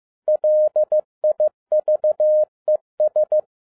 Morse code, heard as a keyed tone, about 15 wpm, 610 Hz. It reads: LIVES
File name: 2011-10-mp3 / n12-JL3YMV 15wpm